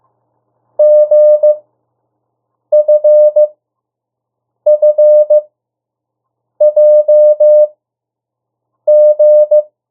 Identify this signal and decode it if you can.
Morse code, heard as a keyed tone, about 15 wpm, spaced as for 6 wpm, 600 Hz. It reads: GFFJG
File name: urok 1